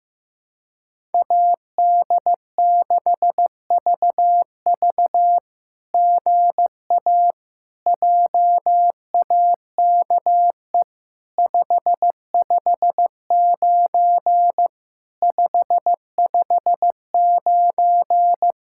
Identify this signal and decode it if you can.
Morse code, heard as a keyed tone, about 15 wpm, 700 Hz. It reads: AD6VV GA JAKE 559 559
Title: SOTA QSO Practice 15wpm